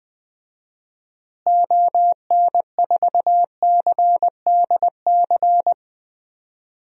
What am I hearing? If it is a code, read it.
Morse code, heard as a keyed tone, about 21 wpm, 700 Hz. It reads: ON4CDC
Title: DXCC-100 Call Signs - 6 Characters - Repeated 3x - 20wpm